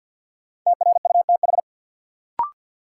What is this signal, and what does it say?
Morse code, heard as a keyed tone, about 46 wpm, 700 Hz. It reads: TRUTH
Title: Sentences from Top 800 Words - New Word Drill - 50wpm